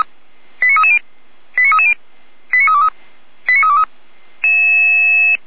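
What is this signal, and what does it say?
Five-tone selective call (ZVEI), heard as quick runs of five tones, tones of about 70 ms. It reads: 89330 89330 89323 89323
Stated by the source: ZVEI five-tone call received by SDR, from the server of feuerwehr-orlamuende.org.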